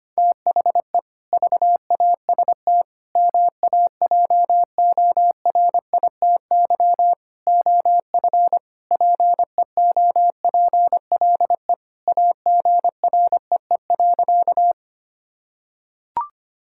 Morse code, heard as a keyed tone, about 25 wpm, 700 Hz. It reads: THE VAST MAJORITY OF PEOPLE AGREE.
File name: practice / 25wpm / Sentences with 4-Word Phrases - 25wpm